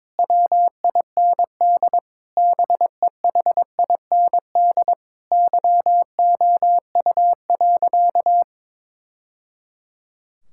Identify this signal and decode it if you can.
Morse code, heard as a keyed tone, about 22 wpm, 700 Hz. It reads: WIND BEHIND YOU.
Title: Sentences from Top 1000 Words - Review Part 6 - 22wpm